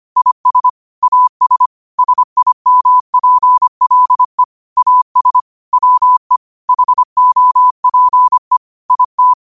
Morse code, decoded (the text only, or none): IS AS SIMPLE AS WE HOPE IT